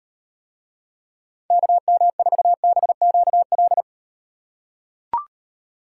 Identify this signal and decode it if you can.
Morse code, heard as a keyed tone, about 38 wpm, 700 Hz. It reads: KM4BQL